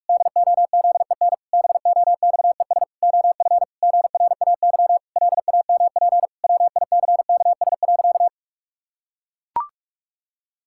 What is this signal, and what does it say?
Morse code, heard as a keyed tone, about 41 wpm, 700 Hz. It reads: DOZEN BOXES OF GRAY LAMP WICKS.
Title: Pangram Challenge v2 - 45wpm